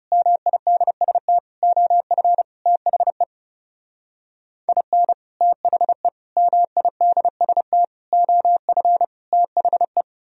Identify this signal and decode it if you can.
Morse code, heard as a keyed tone, about 35 wpm, 700 Hz. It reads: MIDST OF THE IN THE MIDST OF THE
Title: Speed-Racing - Sets of 5 Words 30wpm